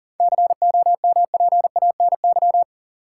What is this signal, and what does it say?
Morse code, heard as a keyed tone, about 40 wpm, 700 Hz. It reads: COMPANY
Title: Seven Letter Words - 40wpm